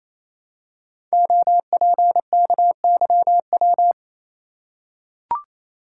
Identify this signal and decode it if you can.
Morse code, heard as a keyed tone, about 29 wpm, 700 Hz. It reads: OPKYW
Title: Code Groups - 5 Letter:Numbers - 28wpm